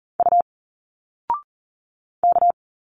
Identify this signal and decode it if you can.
Morse code, heard as a keyed tone, about 40 wpm, 700 Hz. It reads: U K
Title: Single Letters 40wpm